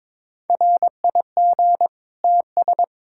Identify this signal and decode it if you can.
Morse code, heard as a keyed tone, about 21 wpm, 700 Hz. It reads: RIG TS
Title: K1USN SST QSO Practice 22wpm